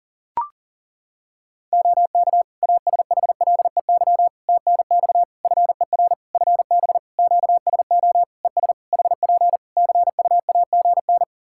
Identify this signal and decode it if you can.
Morse code, heard as a keyed tone, about 40 wpm, 700 Hz. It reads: OK ASHLEY TNX FER FB QSO ES HP CUAGN
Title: Practice QSO - Exchange 4 - Ending - 40wpm